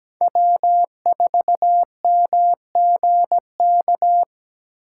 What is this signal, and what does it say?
Morse code, heard as a keyed tone, about 20 wpm, 700 Hz. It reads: W4MGK
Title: Speed Racing - USA 1x3 Call Signs 17wpm